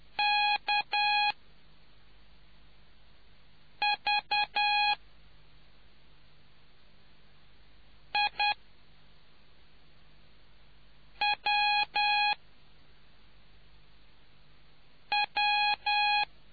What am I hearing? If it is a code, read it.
Morse code, heard as a keyed tone, about 10 wpm, spaced as for 3 wpm, 830 Hz. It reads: KVIWW